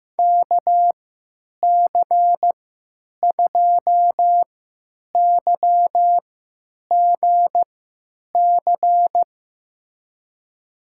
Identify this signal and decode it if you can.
Morse code, heard as a keyed tone, about 16 wpm, spaced as for 9 wpm, 700 Hz. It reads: KC2YGC
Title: USA Call Signs - 6 Characters - 3x Character Spacing - 15wpm